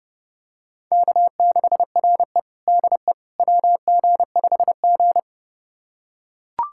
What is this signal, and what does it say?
Morse code, heard as a keyed tone, about 30 wpm, 700 Hz. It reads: K6RE DE WG5G
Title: USA Callsign DE Callsign - 4 Characters - 30wpm